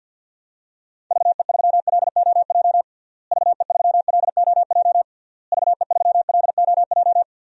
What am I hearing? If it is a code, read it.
Morse code, heard as a keyed tone, about 53 wpm, 700 Hz. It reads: VE3LOJ VE3LOJ VE3LOJ
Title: DXCC-100 Call Signs - 6 Characters - Repeated 3x - 50wpm